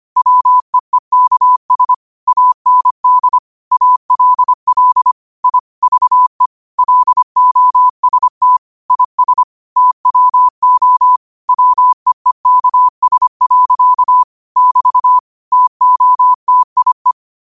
Morse code, heard as a keyed tone, about 25 wpm, 1000 Hz. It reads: WEEKS AND ALL I VE LOST IS TWO WEEKS. <BT> TOTIE
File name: qotdfiles / 25wpm